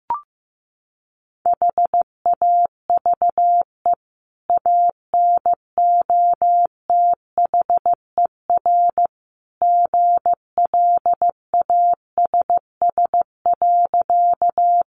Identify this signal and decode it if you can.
Morse code, heard as a keyed tone, about 15 wpm, 700 Hz. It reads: HAVE ANOTHER GLASS.
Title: Sentences from Top 1300 Words 15wpm